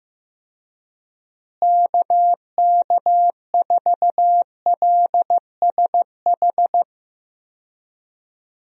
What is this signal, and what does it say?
Morse code, heard as a keyed tone, about 15 wpm, 700 Hz. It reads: KK4LSH
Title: USA Call Signs - 6 Characters - Repeated 3x - 15wpm